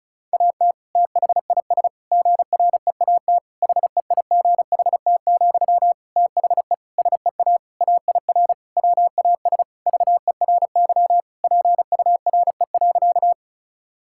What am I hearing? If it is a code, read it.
Morse code, heard as a keyed tone, about 36 wpm, 700 Hz. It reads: AT THIS GREAT HEIGHT, THE SEA AIR WAS VERY PURE.
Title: Sentences from Top 1600 Words 35wpm